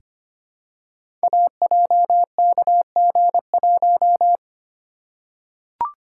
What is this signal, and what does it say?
Morse code, heard as a keyed tone, about 25 wpm, 700 Hz. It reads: AJKG1
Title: Code Groups - 5 Letter:Numbers - 25wpm